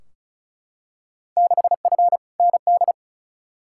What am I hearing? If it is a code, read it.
Morse code, heard as a keyed tone, about 34 wpm, 700 Hz. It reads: 6F ND
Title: Field Day Exchanges Encore - 35wpm